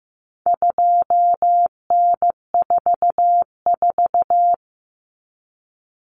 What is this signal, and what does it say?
Morse code, heard as a keyed tone, about 15 wpm, 700 Hz. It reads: 2N44
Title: Common Cut Numbers - 4-Digit 15wpm